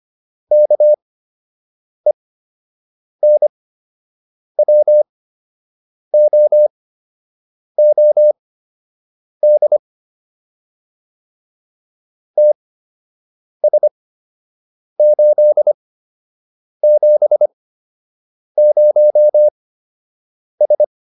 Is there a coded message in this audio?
Yes: Morse code from 0.5 to 20.8 s, heard as a keyed tone, about 25 wpm, spaced as for 7 wpm, 600 Hz. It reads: KENWOOD TS870S